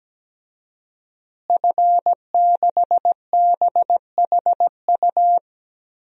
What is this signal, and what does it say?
Morse code, heard as a keyed tone, about 17 wpm, 700 Hz. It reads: F6BHU
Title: DXCC-100 Call Signs - 5 Characters - Repeated 2x - 17wpm